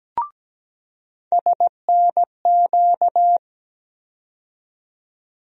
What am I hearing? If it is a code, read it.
Morse code, heard as a keyed tone, about 17 wpm, 700 Hz. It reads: SNQ
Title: USA Call Sign Postfixes - 3 Letters 17wpm